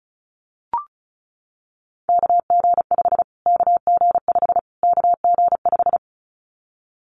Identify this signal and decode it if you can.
Morse code, heard as a keyed tone, about 34 wpm, 700 Hz. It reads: KG5 KG5 KG5
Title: Callsign Prefixes for Top 25 DXCC - Repeated 3x - 35wpm